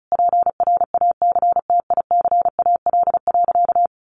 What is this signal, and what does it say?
Morse code, heard as a keyed tone, about 34 wpm, 700 Hz. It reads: PRACTICAL.
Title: Sentences from Top 1400 Words 35wpm